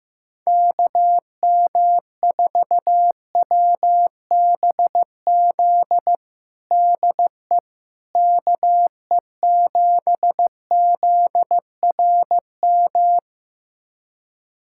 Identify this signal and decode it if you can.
Morse code, heard as a keyed tone, about 15 wpm, 700 Hz. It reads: KM4WBZ DE KE7ZRM
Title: USA Callsign DE Callsign 15wpm